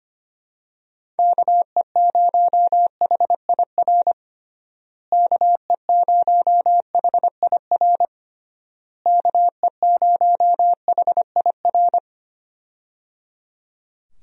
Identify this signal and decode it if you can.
Morse code, heard as a keyed tone, about 25 wpm, 700 Hz. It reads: KE0HIR KE0HIR KE0HIR
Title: USA Call Signs - 6 Characters - Repeated 3x - 25wpm